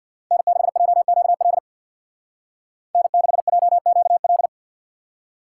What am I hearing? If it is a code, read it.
Morse code, heard as a keyed tone, about 64 wpm, 700 Hz. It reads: N6JQL N6JQL